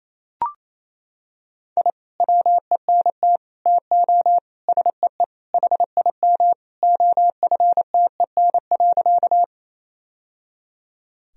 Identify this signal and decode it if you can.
Morse code, heard as a keyed tone, about 27 wpm, 700 Hz. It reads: I WENT TO SEE HIM OFTEN.
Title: Sentences from Top 1000 Words - Review Part 1 - 28wpm